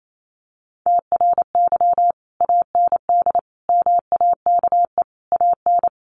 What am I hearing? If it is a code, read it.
Morse code, heard as a keyed tone, about 28 wpm, 700 Hz. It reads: TRY AND MAKE AN
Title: Sentences from Top 1200 Words 28wpm